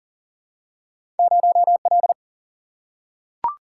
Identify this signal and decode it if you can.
Morse code, heard as a keyed tone, about 42 wpm, 700 Hz. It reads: 0L
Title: Single-Letter Single-Number 40wpm